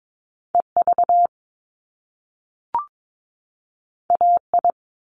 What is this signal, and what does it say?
Morse code, heard as a keyed tone, about 22 wpm, 700 Hz. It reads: EV AI
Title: Top Two Letter Combinations in Words 22wpm